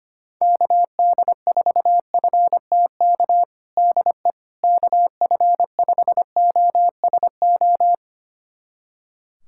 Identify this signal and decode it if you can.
Morse code, heard as a keyed tone, about 25 wpm, 700 Hz. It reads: KD4FTK DE KF5OSO
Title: USA Callsign DE Callsign 25wpm